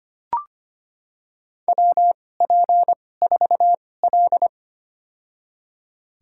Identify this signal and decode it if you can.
Morse code, heard as a keyed tone, about 25 wpm, spaced as for 18 wpm, 700 Hz. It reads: WP4L
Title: USA Call Signs - 4 Characters - 2x Character Spacing - 25wpm